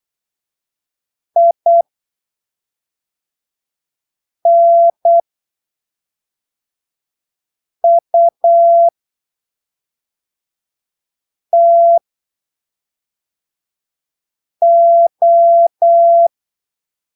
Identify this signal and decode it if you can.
Morse code, heard as a keyed tone, about 8 wpm, spaced as for 3 wpm, 680 Hz. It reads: INUTO
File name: arquivos / texto / texto 29 08ppm